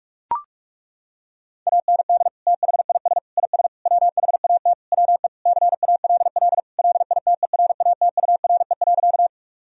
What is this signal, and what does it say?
Morse code, heard as a keyed tone, about 41 wpm, 700 Hz. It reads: AND THIS IS WHAT WE CALL LITERATURE.
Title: Sentences from Top 1000 Words - Review Part 3 - 45wpm